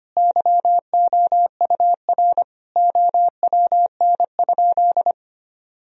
Morse code, heard as a keyed tone, about 25 wpm, 700 Hz. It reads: YOUR OWN?
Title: Sentences from Top 1000 Words - Review Part 4 - 25wpm